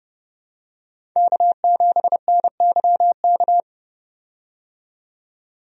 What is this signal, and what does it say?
Morse code, heard as a keyed tone, about 32 wpm, 700 Hz. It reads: K7NYK
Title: USA Call Signs - 5 Characters - 30wpm